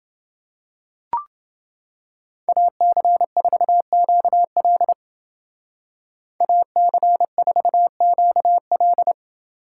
Morse code, heard as a keyed tone, about 29 wpm, 700 Hz. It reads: AC4QL AC4QL